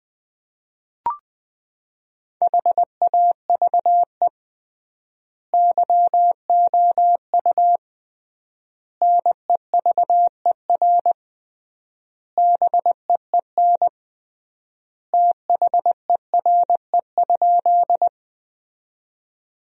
Morse code, heard as a keyed tone, about 20 wpm, 700 Hz. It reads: HAVE YOU NEVER BEEN THERE?
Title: 3x Word-Spacing - Sentences from Top 300 Words - 20wpm